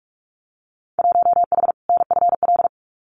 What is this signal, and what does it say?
Morse code, heard as a keyed tone, about 46 wpm, 700 Hz. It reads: 1H NFL
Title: Winter Field Day Exchanges 45wpm